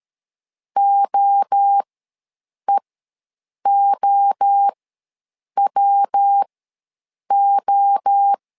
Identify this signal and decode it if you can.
Morse code, heard as a keyed tone, about 13 wpm, spaced as for 7 wpm, 800 Hz. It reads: OEOWO
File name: LEZIONI / 3 MWRO / 3 MWRO ws 12wpm 1914